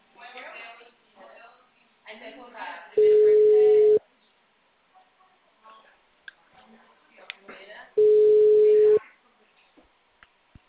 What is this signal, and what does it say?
Ringback tone (Europe, 425 Hz), with speech.
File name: out- -2015- - - .579